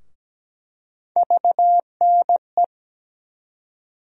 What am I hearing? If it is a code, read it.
Morse code, heard as a keyed tone, about 17 wpm, 700 Hz. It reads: VNE
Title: USA Call Sign Postfixes - 3 Letters 17wpm